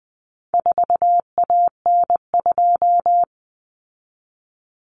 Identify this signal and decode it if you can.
Morse code, heard as a keyed tone, about 20 wpm, 700 Hz. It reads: 4AN2
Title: Common Cut Numbers - 4-Digit 20wpm